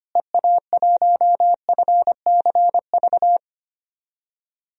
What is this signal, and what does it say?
Morse code, heard as a keyed tone, about 25 wpm, 700 Hz. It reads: EA1FCV
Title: DXCC-100 Call Signs - 6 Characters - Repeated 2x - 25wpm